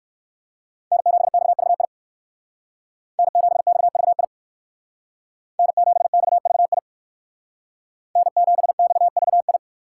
Morse code, heard as a keyed tone, about 58 wpm, 700 Hz. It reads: N7XVI N7XVI N7XVI N7XVI